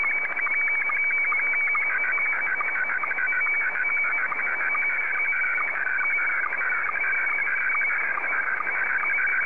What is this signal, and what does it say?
SSTV (slow-scan television), Scottie 1.